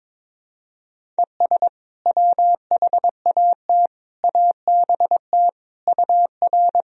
Morse code, heard as a keyed tone, about 22 wpm, 700 Hz. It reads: ES WHAT ABT UR
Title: Sentences from Top 200 Words and Abbreviations 22wpm